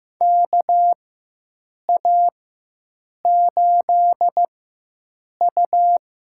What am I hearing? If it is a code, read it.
Morse code, heard as a keyed tone, about 15 wpm, spaced as for 7 wpm, 700 Hz. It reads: KA8U
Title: USA Call Signs - 4x Character Spacing - 15wpm